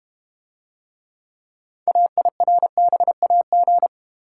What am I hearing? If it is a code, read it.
Morse code, heard as a keyed tone, about 33 wpm, 700 Hz. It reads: AIRBAG